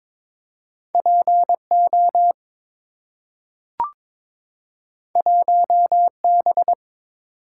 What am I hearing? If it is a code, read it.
Morse code, heard as a keyed tone, about 22 wpm, 700 Hz. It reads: PO 1B
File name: practice / 22wpm / Two Letter-Number Combinations - 22wpm